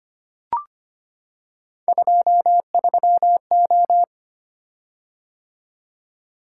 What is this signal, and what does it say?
Morse code, heard as a keyed tone, about 25 wpm, 700 Hz. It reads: 23O
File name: Three Letter-Number Combinations - 25wpm